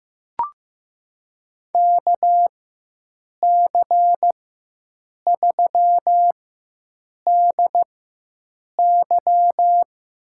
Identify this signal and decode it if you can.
Morse code, heard as a keyed tone, about 15 wpm, spaced as for 7 wpm, 700 Hz. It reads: KC3DY
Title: USA Call Signs - 4x Character Spacing - 15wpm